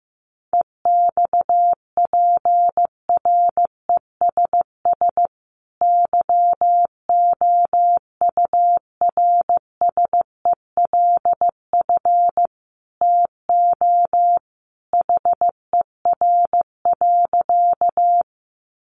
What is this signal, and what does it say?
Morse code, heard as a keyed tone, about 15 wpm, 700 Hz. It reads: EXPRESS YOURSELF TO HER.
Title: Sentences from Top 1100 Words 15wpm